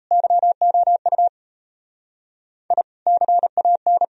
Morse code, heard as a keyed tone, about 39 wpm, 700 Hz. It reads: YOU I CAN